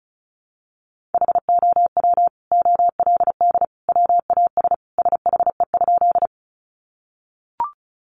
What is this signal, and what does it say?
Morse code, heard as a keyed tone, about 34 wpm, 700 Hz. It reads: HOW OLD WAS SHE?
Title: Sentences from Top 200 Words 35wpm